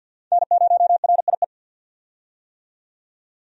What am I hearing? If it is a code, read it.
Morse code, heard as a keyed tone, about 50 wpm, 700 Hz. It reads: N0RIE
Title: USA Call Signs - 5 Characters - 50wpm